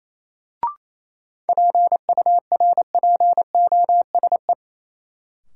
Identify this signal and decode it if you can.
Morse code, heard as a keyed tone, about 28 wpm, 700 Hz. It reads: PURPOSE
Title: Sentences from Top 800 Words - New Word Drill - 28wpm